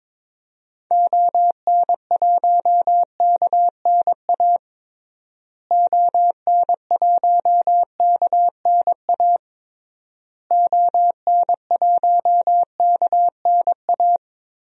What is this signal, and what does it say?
Morse code, heard as a keyed tone, about 22 wpm, 700 Hz. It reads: ON1KNA ON1KNA ON1KNA
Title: DXCC-100 Call Signs - 6 Characters - Repeated 3x - 22wpm